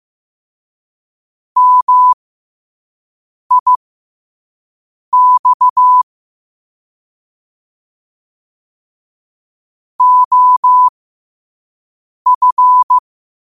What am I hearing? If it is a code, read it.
Morse code, heard as a keyed tone, about 15 wpm, spaced as for 5 wpm, 1000 Hz. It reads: MIX OF
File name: qotdfiles / 5wpm